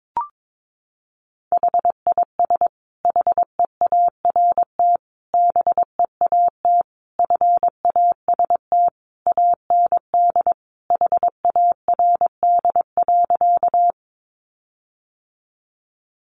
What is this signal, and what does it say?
Morse code, heard as a keyed tone, about 22 wpm, 700 Hz. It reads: HIS HEART BEAT FAST AND HARD.
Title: Sentences from Top 1300 Words 22wpm